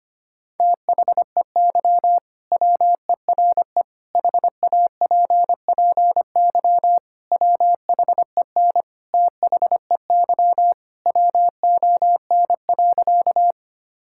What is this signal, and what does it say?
Morse code, heard as a keyed tone, about 25 wpm, 700 Hz. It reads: THEY WERE HAPPY WHEN THEY WON.
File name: practice / 25wpm / Sentences with 2-Word Phrases - 25wpm